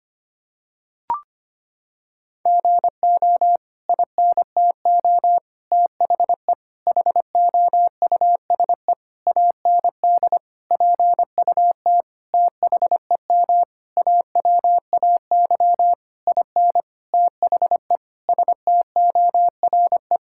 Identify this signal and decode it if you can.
Morse code, heard as a keyed tone, about 25 wpm, 700 Hz. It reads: GO INTO THE HOUSE AND PUT THEM AWAY IN THE STORE
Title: Sentences from Top 1000 Words - Review Part 5 - 25wpm